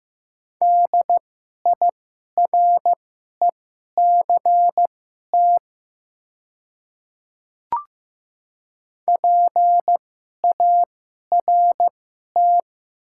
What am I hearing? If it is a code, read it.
Morse code, heard as a keyed tone, about 15 wpm, spaced as for 11 wpm, 690 Hz. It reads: DIRECT PART
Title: 2x Character Spacing - Top 500 Words - 15wpm